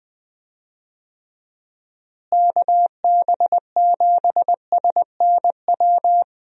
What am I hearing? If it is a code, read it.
Morse code, heard as a keyed tone, about 20 wpm, 700 Hz. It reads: KB7SNW
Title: USA Call Signs - 6 Characters - Repeated 3x - 20wpm